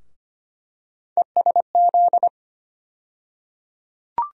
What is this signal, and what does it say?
Morse code, heard as a keyed tone, about 25 wpm, 700 Hz. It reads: ESZ
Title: USA Call Sign Postfixes - 3 Letters 25wpm